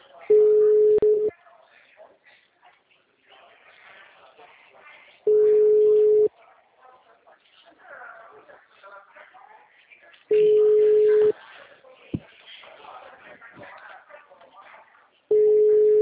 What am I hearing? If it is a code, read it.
Ringback tone (Europe, 425 Hz), with speech.